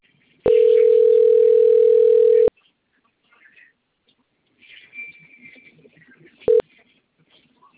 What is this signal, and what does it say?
Ringback tone (North America), 440 + 480 Hz.
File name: out- - - - - .25802